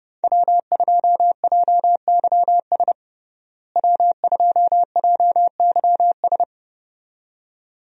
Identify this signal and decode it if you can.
Morse code, heard as a keyed tone, about 30 wpm, 700 Hz. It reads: W2JYS W2JYS
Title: USA Call Signs - 5 Characters - Repeated 2x - 30wpm